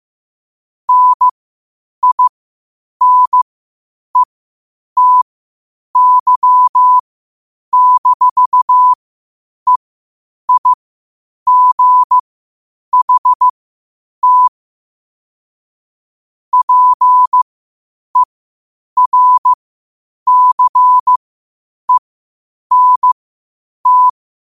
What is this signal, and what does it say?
Morse code, heard as a keyed tone, about 15 wpm, spaced as for 8 wpm, 1000 Hz. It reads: NINETY-EIGHT PERCENT
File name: qotdfiles / 8wpm